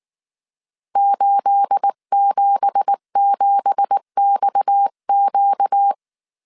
Morse code, heard as a keyed tone, about 19 wpm, 800 Hz. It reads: 877XQ
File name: CW / DET / 1 9 1500C S B 18wpm 2551